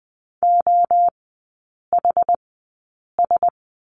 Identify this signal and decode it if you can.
Morse code, heard as a keyed tone, about 20 wpm, spaced as for 8 wpm, 700 Hz. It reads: OHS
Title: 067 Lesson 12e - Letter d - Three Characters 20wpm